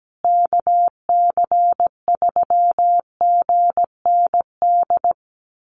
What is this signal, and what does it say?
Morse code, heard as a keyed tone, about 17 wpm, 700 Hz. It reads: KC3GND
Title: USA Call Signs Repeated 2x 17wpm